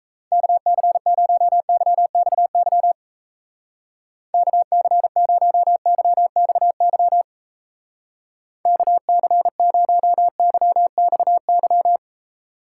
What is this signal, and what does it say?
Morse code, heard as a keyed tone, about 34 wpm, 700 Hz. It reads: KC0YXY KC0YXY KC0YXY
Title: Speed Racing - USA Call Signs 28wpm